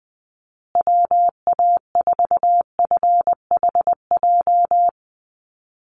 Morse code, heard as a keyed tone, about 20 wpm, 700 Hz. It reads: WA4FHJ
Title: USA Call Signs Repeated 2x 20wpm